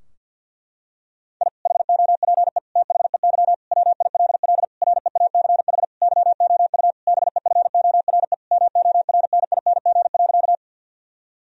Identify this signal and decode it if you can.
Morse code, heard as a keyed tone, about 52 wpm, 700 Hz. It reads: I HOPE THEY WILL REACH YOU BEFORE MORNING.
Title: Sentences from Top 1000 Words - Review Part 2 - 50wpm